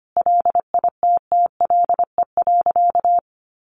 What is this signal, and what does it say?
Morse code, heard as a keyed tone, about 25 wpm, 700 Hz. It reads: LITTLE.
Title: Sentences from Top 1500 Words 25wpm